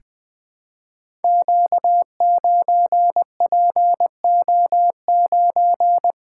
Morse code, heard as a keyed tone, about 20 wpm, 700 Hz. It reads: Q9PO9
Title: Code Groups - 5 Letter:Numbers - 20wpm